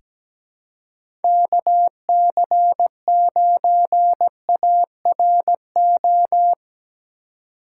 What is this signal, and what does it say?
Morse code, heard as a keyed tone, about 17 wpm, 700 Hz. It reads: KC9ARO
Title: USA Call Signs - 4x Character Spacing - 17wpm